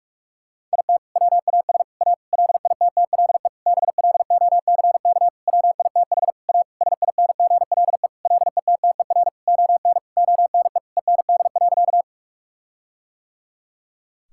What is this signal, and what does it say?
Morse code, heard as a keyed tone, about 46 wpm, 700 Hz. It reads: IT WAS A LITTLE BLOCK WITH A SINGLE LETTER ON ONE END.
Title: Sentences from Top 1600 Words 45wpm